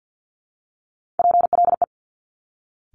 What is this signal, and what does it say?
Morse code, heard as a keyed tone, about 50 wpm, 700 Hz. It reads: PLE
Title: Top Three Letter Combinations in Words 50wpm